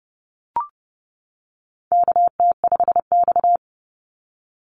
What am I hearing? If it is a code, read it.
Morse code, heard as a keyed tone, about 30 wpm, 700 Hz. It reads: KT5X
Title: NA SOTA Activator Callsigns 30wpm